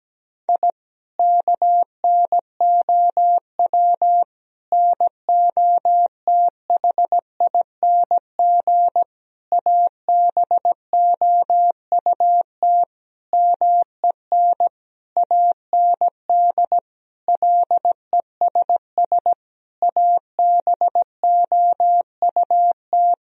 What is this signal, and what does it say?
Morse code, heard as a keyed tone, about 17 wpm, 700 Hz. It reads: I KNOW NOTHING ABOUT MEN AND LESS ABOUT
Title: Sentences from Top 1000 Words - Review Part 2 - 17wpm